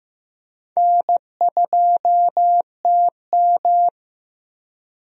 Morse code, heard as a keyed tone, about 16 wpm, 700 Hz. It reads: N2TM
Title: USA Call Signs - 4 Characters - Repeated 3x - 15wpm